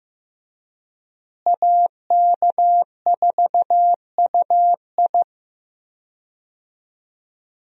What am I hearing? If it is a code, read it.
Morse code, heard as a keyed tone, about 15 wpm, 700 Hz. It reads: AK4UI
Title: USA Call Signs - 5 Characters - Repeated 3x - 15wpm